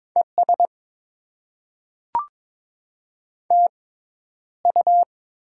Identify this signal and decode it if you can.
Morse code, heard as a keyed tone, about 22 wpm, spaced as for 11 wpm, 700 Hz. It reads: ES TU
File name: Top 90 QSO Elements - 6x Character Spacing - 22wpm